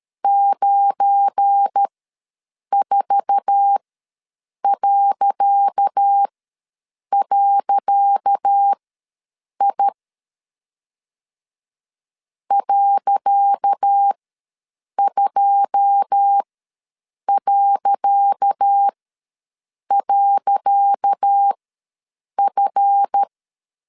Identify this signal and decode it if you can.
Morse code, heard as a keyed tone, about 13 wpm, spaced as for 7 wpm, 800 Hz. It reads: 94..I .2..F
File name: LEZIONI / 10 SIGNS / 10C DOT ws 12wpm 3548